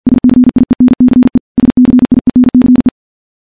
Morse code, heard as a keyed tone, about 51 wpm, 250 Hz. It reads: SCIENCE SCIENCE
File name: morse / morse-training-50WPM